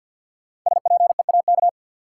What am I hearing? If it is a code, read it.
Morse code, heard as a keyed tone, about 52 wpm, 700 Hz. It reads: SPEAK
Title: Sentences from Top 1000 Words - Review Part 2 - New Word Drill - 50wpm